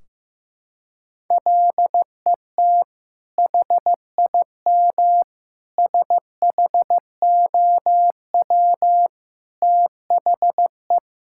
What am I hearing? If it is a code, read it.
Morse code, heard as a keyed tone, about 15 wpm, 700 Hz. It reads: LET HIM SHOW THE
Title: Sentences from Top 1600 Words 15wpm